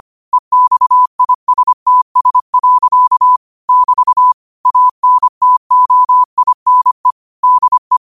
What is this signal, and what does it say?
Morse code, heard as a keyed tone, about 25 wpm, 1000 Hz. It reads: EXISTS. <BT> ANTOINE DE